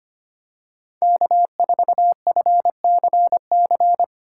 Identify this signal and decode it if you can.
Morse code, heard as a keyed tone, about 25 wpm, 700 Hz. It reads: K4FCC
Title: K1USN SST QSO Practice - 25wpm